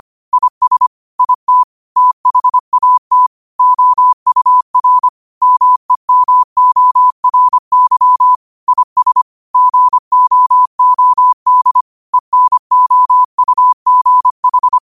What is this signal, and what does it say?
Morse code, heard as a keyed tone, about 25 wpm, 1000 Hz. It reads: IS IT THAT OUR MEMORY IS GOOD ENOUGH